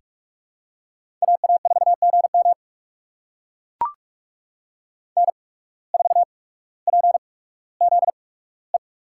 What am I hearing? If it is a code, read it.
Morse code, heard as a keyed tone, about 42 wpm, spaced as for 19 wpm, 700 Hz. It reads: AA4GM N4PZE
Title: USA Call Signs - 5 Characters - 8x Character Spacing - 45wpm